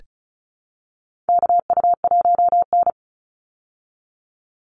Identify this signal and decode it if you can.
Morse code, heard as a keyed tone, about 34 wpm, 700 Hz. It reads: KU1N
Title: Call Signs Repeated 2x - 35wpm